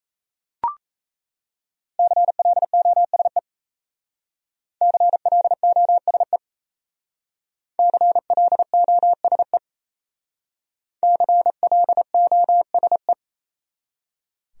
Morse code, heard as a keyed tone, about 35 wpm, 700 Hz. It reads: CLOSE CLOSE CLOSE CLOSE
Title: Speed-Racing Top 500 Words 28wpm